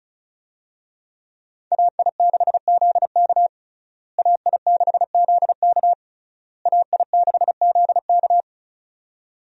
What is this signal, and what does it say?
Morse code, heard as a keyed tone, about 36 wpm, 700 Hz. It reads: AI6ZK AI6ZK AI6ZK
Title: USA Call Signs - 5 Characters - Repeated 3x - 35wpm